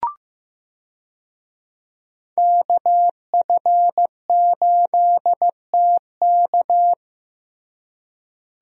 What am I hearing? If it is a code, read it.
Morse code, heard as a keyed tone, about 15 wpm, 700 Hz. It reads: KF8TK
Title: USA Call Signs - 5 Characters - Repeated 3x - 15wpm